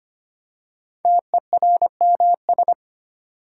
Morse code, heard as a keyed tone, about 25 wpm, 700 Hz. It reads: TERMS
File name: 5 Letter Words - First 3 Letters Given - 25wpm